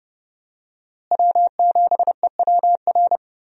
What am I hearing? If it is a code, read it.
Morse code, heard as a keyed tone, about 30 wpm, 700 Hz. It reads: W7EWR
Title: USA Call Signs - 5 Characters - Repeated 2x - 30wpm